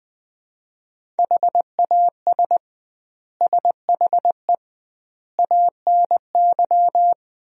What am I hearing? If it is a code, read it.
Morse code, heard as a keyed tone, about 20 wpm, 700 Hz. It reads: HAS SHE ANY
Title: 2x Word-Spacing - Sentences from Top 400 Words - 20wpm